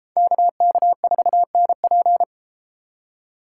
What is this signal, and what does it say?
Morse code, heard as a keyed tone, about 33 wpm, 700 Hz. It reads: KK4NP